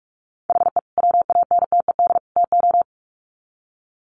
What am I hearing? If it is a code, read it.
Morse code, heard as a keyed tone, about 44 wpm, 700 Hz. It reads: HE WANTED TO
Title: Sets of 3 Words 45wpm